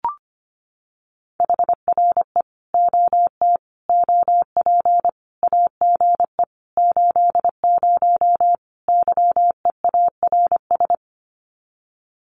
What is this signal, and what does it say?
Morse code, heard as a keyed tone, about 25 wpm, 700 Hz. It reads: HRE OT OP AGE 80 YEARS
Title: Common QSO Phrases 25wpm